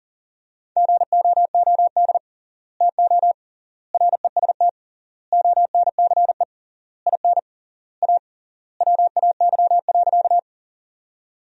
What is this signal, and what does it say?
Morse code, heard as a keyed tone, about 40 wpm, 700 Hz. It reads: GOOD TO REST ONCE IN A WAY.
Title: 3x Word-Spacing - Sentences from Top 500 Words - 40wpm